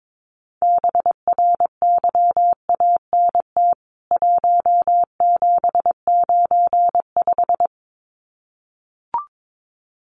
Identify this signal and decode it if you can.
Morse code, heard as a keyed tone, about 22 wpm, 700 Hz. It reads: BRYANT 1795
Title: CWT Exchange 22wpm